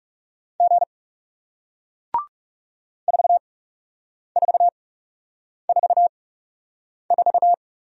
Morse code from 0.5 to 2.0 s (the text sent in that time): G